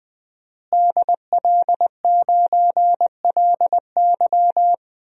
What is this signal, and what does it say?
Morse code, heard as a keyed tone, about 20 wpm, 700 Hz. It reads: DL9LY
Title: DXCC-100 Call Signs - 5 Characters - Repeated 2x - 20wpm